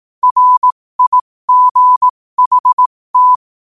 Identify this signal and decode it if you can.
Morse code, heard as a keyed tone, about 18 wpm, spaced as for 15 wpm, 1000 Hz. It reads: RIGHT
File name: qotdfiles / 15wpm